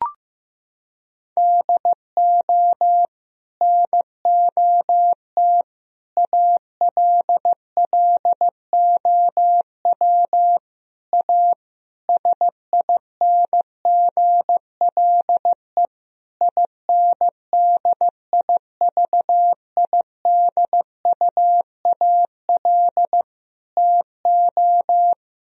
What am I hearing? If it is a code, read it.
Morse code, heard as a keyed tone, about 15 wpm, 700 Hz. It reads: DO NOT ALLOW A SINGLE INDIVIDUAL TO
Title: Sentences from Top 1000 Words - Review Part 4 - 15wpm